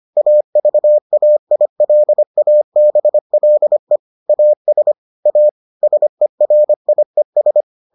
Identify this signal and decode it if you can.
Morse code, heard as a keyed tone, about 25 wpm, 600 Hz. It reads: AVAILABLE AS A SERIES